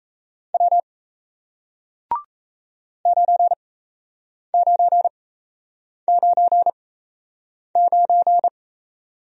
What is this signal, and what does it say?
Morse code, heard as a keyed tone, about 38 wpm, spaced as for 8 wpm, 700 Hz. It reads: W 9999